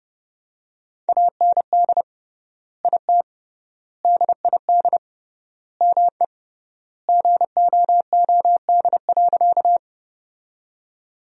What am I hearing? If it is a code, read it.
Morse code, heard as a keyed tone, about 32 wpm, 700 Hz. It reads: AND IT DID ME GOOD.
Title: 3x Word-Spacing - Sentences from Top 200 Words - 30wpm